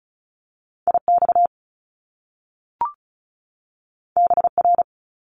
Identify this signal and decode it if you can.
Morse code, heard as a keyed tone, about 36 wpm, 700 Hz. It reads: IX BR
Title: Top Two Letter Combinations in Words 35wpm